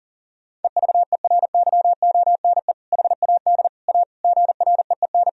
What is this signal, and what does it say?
Morse code, heard as a keyed tone, about 40 wpm, 700 Hz. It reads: EVERYONE HAD A GREEN
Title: Sentences from Top 1000 Words - Review Part 2 - 40wpm